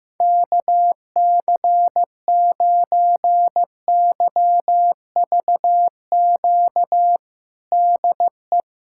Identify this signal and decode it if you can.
Morse code, heard as a keyed tone, about 15 wpm, 700 Hz. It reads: KC9YVQ DE
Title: USA Callsign DE Callsign - 6 Characters - 15wpm